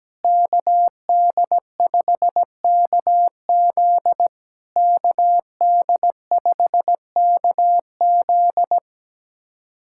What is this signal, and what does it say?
Morse code, heard as a keyed tone, about 17 wpm, 700 Hz. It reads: KD5KZ KD5KZ
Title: SOTA QSO Practice 17wpm